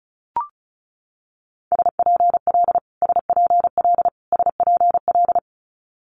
Morse code, heard as a keyed tone, about 36 wpm, 700 Hz. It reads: SPL SPL SPL
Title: Multi-letter Phonemes Repeated 3x 35wpm